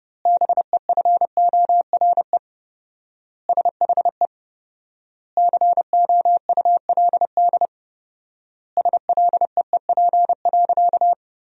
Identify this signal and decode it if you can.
Morse code, heard as a keyed tone, about 32 wpm, 700 Hz. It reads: BEFORE SHE COULD SLEEP.
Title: 4x Word-Spacing - Sentences from Top 500 Words - 30wpm